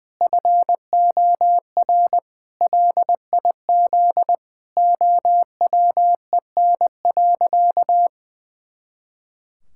Morse code, heard as a keyed tone, about 20 wpm, 700 Hz. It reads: FOR LIZ OWEN.
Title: Pangram Challenge v2 - 20wpm